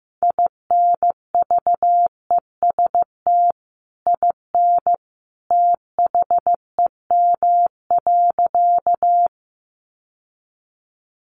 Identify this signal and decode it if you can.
Morse code, heard as a keyed tone, about 15 wpm, 700 Hz. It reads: INVEST IN THEM.
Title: Sentences from Top 1100 Words 15wpm